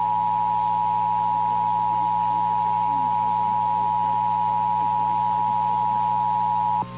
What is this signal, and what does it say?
Emergency Alert System (SAME) tones (the 853 and 960 Hz attention tone), with noise. It is heard from 0.5 to 6.5 s.
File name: eas / KNCYKBIE / KNCYKBIE